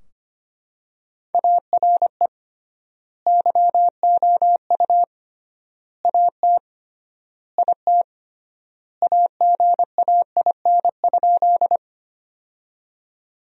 Morse code, heard as a keyed tone, about 25 wpm, 700 Hz. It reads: ARE YOU AT IT AGAIN?
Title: 3x Word-Spacing - Sentences from Top 200 Words - 25wpm